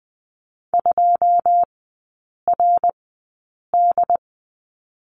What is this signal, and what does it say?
Morse code, heard as a keyed tone, about 20 wpm, spaced as for 8 wpm, 690 Hz. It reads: 2RD
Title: 081 Lesson 14e - Number 2 - Three Characters 20wpm